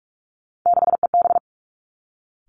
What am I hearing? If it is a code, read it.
Morse code, heard as a keyed tone, about 46 wpm, 700 Hz. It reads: 6EB
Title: Three Challenging Letter-Number Combinations 45wpm